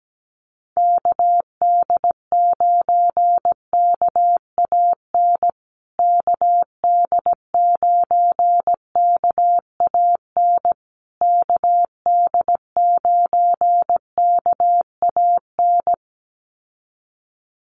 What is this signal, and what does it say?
Morse code, heard as a keyed tone, about 17 wpm, 700 Hz. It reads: KD9KAN KD9KAN KD9KAN
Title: USA Call Signs Repeated 3x 17wpm